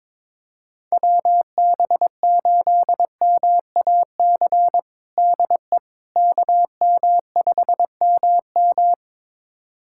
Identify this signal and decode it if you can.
Morse code, heard as a keyed tone, about 22 wpm, 700 Hz. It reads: WB8MAC DE KM5MM
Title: USA Callsign DE Callsign 22wpm